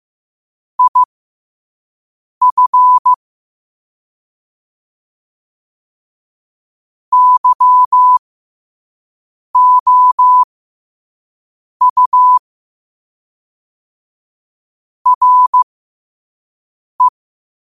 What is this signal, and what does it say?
Morse code, heard as a keyed tone, about 15 wpm, spaced as for 5 wpm, 1000 Hz. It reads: IF YOU RE
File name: qotdfiles / 5wpm